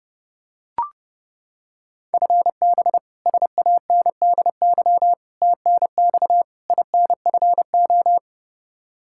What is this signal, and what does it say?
Morse code, heard as a keyed tone, about 30 wpm, 700 Hz. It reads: FB SANDY TNX INFO
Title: Practice QSO - Exchange 3 - More Info - 30wpm